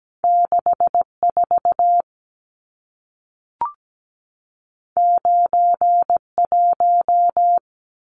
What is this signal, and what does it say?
Morse code, heard as a keyed tone, about 17 wpm, 700 Hz. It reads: 64 91
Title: Numbers - 2-Digit 17wpm